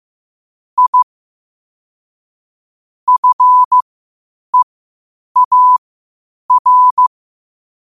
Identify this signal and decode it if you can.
Morse code, heard as a keyed tone, about 15 wpm, spaced as for 8 wpm, 1000 Hz. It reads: I FEAR
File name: qotdfiles / 8wpm